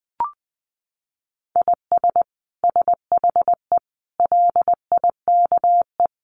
Morse code, heard as a keyed tone, about 20 wpm, 700 Hz. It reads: IS SHE LIKE
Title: Sentences from Top 100 Words 20wpm